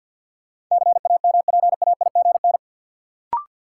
Morse code, heard as a keyed tone, about 51 wpm, 700 Hz. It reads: CAMPAIGN
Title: Sentences from Top 1200 Words - New Word Drill Alt - 50wpm